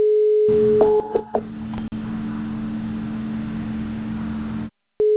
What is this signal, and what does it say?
Ringback tone (Europe, 425 Hz), with noise.